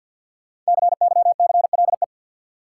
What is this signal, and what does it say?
Morse code, heard as a keyed tone, about 52 wpm, 700 Hz. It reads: CYCLE
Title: Sentences from Top 900 Words - New Word Drill - 50wpm